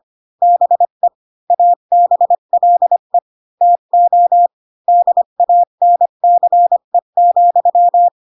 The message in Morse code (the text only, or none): BE ABLE TO DANCE,